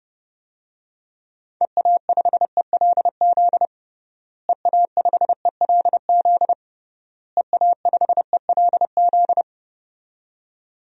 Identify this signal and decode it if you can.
Morse code, heard as a keyed tone, about 32 wpm, 700 Hz. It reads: EA5ELZ EA5ELZ EA5ELZ
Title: DXCC-100 Call Signs - 6 Characters - Repeated 3x - 30wpm